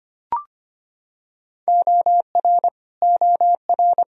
Morse code, heard as a keyed tone, about 25 wpm, 700 Hz. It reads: OR OR
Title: Top 100 Words Repeated 25wpm